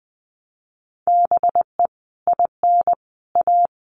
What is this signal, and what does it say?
Morse code, heard as a keyed tone, about 20 wpm, 700 Hz. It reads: BE IN A
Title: Sets of 5 Words Encore2 20wpm